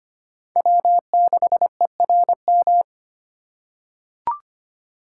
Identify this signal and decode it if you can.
Morse code, heard as a keyed tone, about 25 wpm, 700 Hz. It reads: W6ERM
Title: USA Call Signs - 5 Characters - 4x Character Spacing - 25wpm